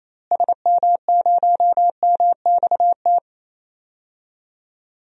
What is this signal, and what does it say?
Morse code, heard as a keyed tone, about 28 wpm, 700 Hz. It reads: SM0MXT
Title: DXCC-100 Call Signs - 6 Characters - Repeated 3x - 28wpm